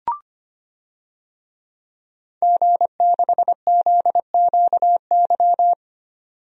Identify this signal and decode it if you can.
Morse code, heard as a keyed tone, about 25 wpm, 700 Hz. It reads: G6ZQY
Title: DXCC-100 Call Signs - 5 Characters - 25wpm